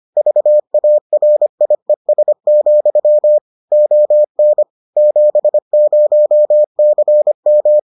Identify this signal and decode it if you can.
Morse code, heard as a keyed tone, about 25 wpm, 600 Hz. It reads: VARIES, ON 70CM